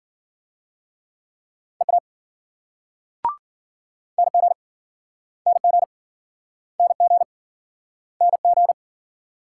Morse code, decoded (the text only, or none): EA NG NG NG NG